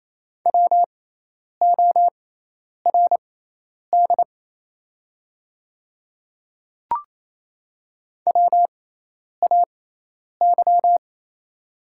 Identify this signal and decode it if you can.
Morse code, heard as a keyed tone, about 27 wpm, spaced as for 10 wpm, 700 Hz. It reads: WORD WAY
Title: 6x Character Spacing - Top 200 Words - 28wpm